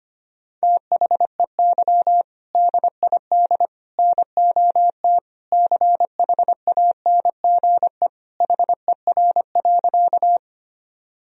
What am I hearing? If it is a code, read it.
Morse code, heard as a keyed tone, about 25 wpm, 700 Hz. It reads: THEY DID NOT CHANGE HER.
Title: Sentences from Top 1000 Words - Review Part 1 - 25wpm